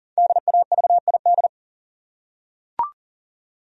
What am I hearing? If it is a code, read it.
Morse code, heard as a keyed tone, about 40 wpm, 700 Hz. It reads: DAVID
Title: USA Common Names - Max 5 Letters - 40wpm